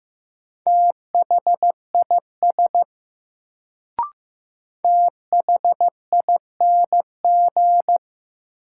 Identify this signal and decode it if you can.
Morse code, heard as a keyed tone, about 15 wpm, 700 Hz. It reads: THIS THING
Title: Sentences from Top 100 Words - New Word Drill - 15wpm